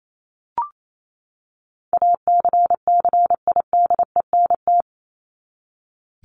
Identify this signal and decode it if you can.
Morse code, heard as a keyed tone, about 28 wpm, 700 Hz. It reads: ACCIDENT
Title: Top 1751-2000 Words 28wpm